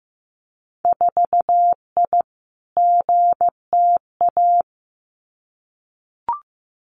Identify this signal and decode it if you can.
Morse code, heard as a keyed tone, about 15 wpm, 700 Hz. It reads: 4I GTA
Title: Winter Field Day Exchanges Encore 15wpm